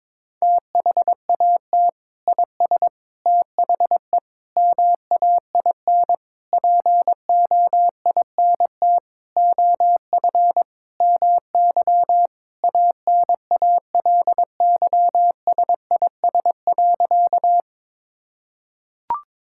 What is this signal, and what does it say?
Morse code, heard as a keyed tone, about 22 wpm, 700 Hz. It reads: THAT IS THE MAIN POINT OF MY ANALYSIS.
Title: Sentences from Top 1000 Words - Review Part 5 - 22wpm